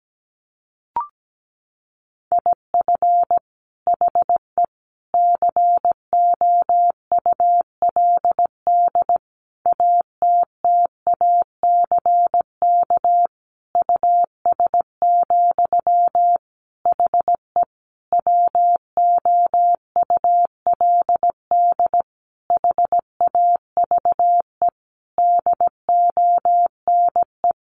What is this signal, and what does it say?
Morse code, heard as a keyed tone, about 17 wpm, 700 Hz. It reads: IF HE COULD ATTACK US, HE WOULD HAVE DONE
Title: Sentences from Top 1500 Words 17wpm